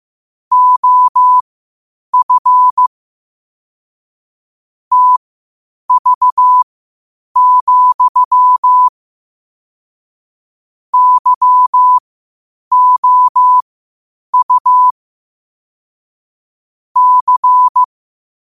Morse code, heard as a keyed tone, about 15 wpm, spaced as for 8 wpm, 1000 Hz. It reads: OF TV, YOU C
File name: qotdfiles / 8wpm